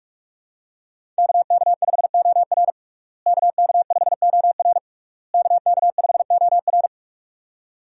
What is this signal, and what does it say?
Morse code, heard as a keyed tone, about 46 wpm, 700 Hz. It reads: KK5OR KK5OR KK5OR
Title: USA Call Signs - 5 Characters - Repeated 3x - 45wpm